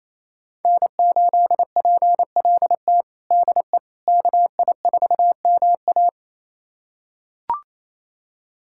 Morse code, heard as a keyed tone, about 28 wpm, 700 Hz. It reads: N8PLT DE KI4MA
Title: USA Callsign DE Callsign - 5 Characters - 28wpm